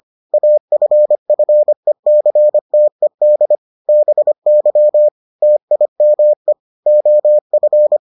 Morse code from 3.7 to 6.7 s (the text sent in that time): BY TIME